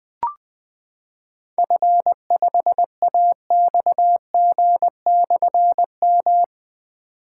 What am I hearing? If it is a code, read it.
Morse code, heard as a keyed tone, about 20 wpm, 700 Hz. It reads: F5AXG/M
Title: Difficult Call Signs 20wpm